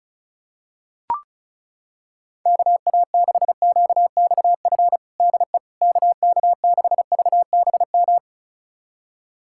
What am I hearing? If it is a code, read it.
Morse code, heard as a keyed tone, about 36 wpm, 700 Hz. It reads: KA6QXF DE KK6VBM
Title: USA Callsign DE Callsign - 6 Characters - 35wpm